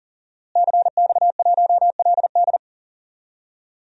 Morse code, heard as a keyed tone, about 40 wpm, 700 Hz. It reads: CX1LD
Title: DXCC-100 Call Signs - 5 Characters - Repeated 3x - 40wpm